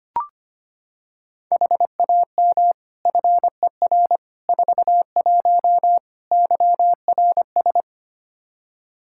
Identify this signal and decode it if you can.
Morse code, heard as a keyed tone, about 25 wpm, 700 Hz. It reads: HAM FER 41 YRS
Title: K1USN SST QSO Practice 25wpm